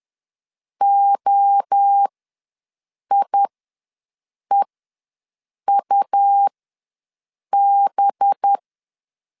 Morse code, heard as a keyed tone, about 10 wpm, spaced as for 6 wpm, 800 Hz. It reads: OIEUB
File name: LEZIONI / 5 LBVC / 5 LBVC ws 10wpm 3733